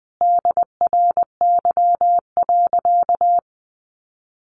Morse code, heard as a keyed tone, about 20 wpm, 700 Hz. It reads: DRY.
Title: Sentences from Top 1500 Words 20wpm